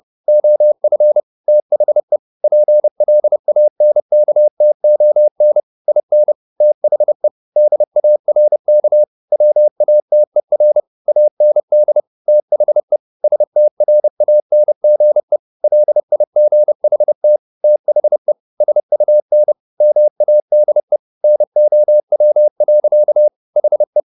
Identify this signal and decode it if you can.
Morse code, heard as a keyed tone, about 30 wpm, 600 Hz. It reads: OF THE PLANKTON IN THE DARK WATER AND THE STRANGE LIGHT THE SUN MADE NOW. HE
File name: the old man and the sea / the old man and the sea 1 30wpm